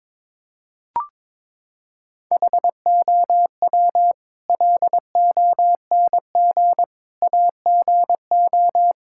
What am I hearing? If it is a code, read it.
Morse code, heard as a keyed tone, about 22 wpm, 700 Hz. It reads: HOW LONG AGO
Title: Sentences from Top 1000 Words - Review Part 6 - 22wpm